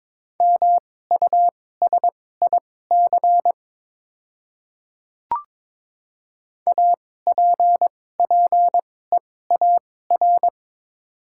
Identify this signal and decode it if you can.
Morse code, heard as a keyed tone, about 22 wpm, spaced as for 16 wpm, 700 Hz. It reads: MUSIC APPEAR
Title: 2x Character Spacing - Top 500 Words - 22wpm